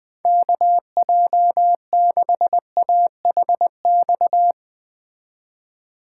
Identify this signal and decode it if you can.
Morse code, heard as a keyed tone, about 21 wpm, 700 Hz. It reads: KJ6AHX
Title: Speed Racing - USA 2x3 Call Signs 17wpm